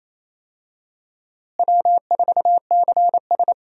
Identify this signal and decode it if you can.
Morse code, heard as a keyed tone, about 28 wpm, 700 Hz. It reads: W4CS
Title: USA Call Signs - 4 Characters - Repeated 2x - 28wpm